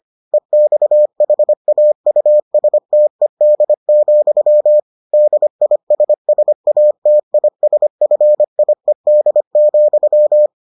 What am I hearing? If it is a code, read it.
Morse code, heard as a keyed tone, about 25 wpm, 600 Hz. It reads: EXHAUSTED, DISSATISFIED,